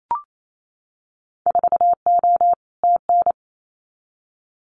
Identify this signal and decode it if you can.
Morse code, heard as a keyed tone, about 28 wpm, 700 Hz. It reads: 4O TN
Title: Winter Field Day Exchanges Encore 28wpm